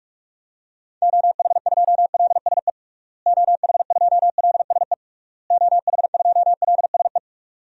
Morse code, heard as a keyed tone, about 46 wpm, 700 Hz. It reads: OH2LSE OH2LSE OH2LSE
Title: DXCC-100 Call Signs - 6 Characters - Repeated 3x - 45wpm